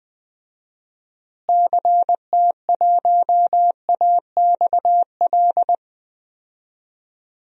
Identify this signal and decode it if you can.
Morse code, heard as a keyed tone, about 21 wpm, 700 Hz. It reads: CT1AXL
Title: DXCC-100 Call Signs - 6 Characters - Repeated 3x - 20wpm